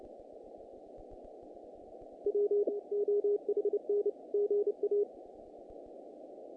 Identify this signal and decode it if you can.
Morse code, heard as a keyed tone, about 29 wpm, 410 Hz. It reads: W OHN GA
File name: contests / naqp-cw-k5tr-op-k5pi- -2100